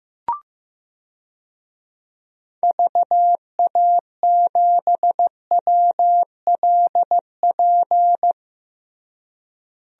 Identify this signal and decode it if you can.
Morse code, heard as a keyed tone, about 15 wpm, 700 Hz. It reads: VA7WLP
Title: DXCC-100 Call Signs - 6 Characters - Repeated 2x - 15wpm